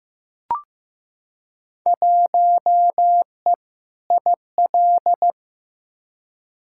Morse code, heard as a keyed tone, about 15 wpm, 700 Hz. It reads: 1E IL
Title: Field Day Exchanges - 15wpm